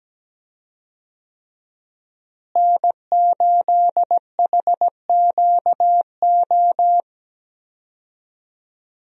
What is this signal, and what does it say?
Morse code, heard as a keyed tone, about 17 wpm, 700 Hz. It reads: N8HQO
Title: USA Call Signs - 5 Characters - Repeated 3x - 17wpm